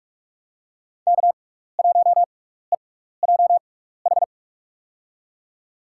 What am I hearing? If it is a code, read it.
Morse code, heard as a keyed tone, about 46 wpm, spaced as for 16 wpm, 700 Hz. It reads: K1EJH
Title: USA Call Signs - 6x Character Spacing - 45wpm